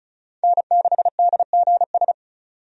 Morse code, heard as a keyed tone, about 34 wpm, 700 Hz. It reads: N6DGS
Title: USA Call Signs - 5 Characters - 35wpm